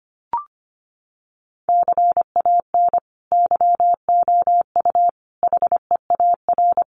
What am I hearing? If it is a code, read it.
Morse code, heard as a keyed tone, about 25 wpm, 700 Hz. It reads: CAN YOU HEAR
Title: Sentences from Top 300 Words 25wpm